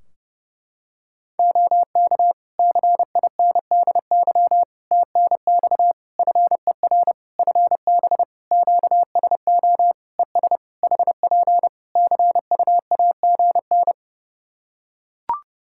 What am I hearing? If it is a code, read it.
Morse code, heard as a keyed tone, about 30 wpm, 700 Hz. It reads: OK CINDY TNX FER FB QSO ES HP CUAGN
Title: K1USN SST QSO Practice 30wpm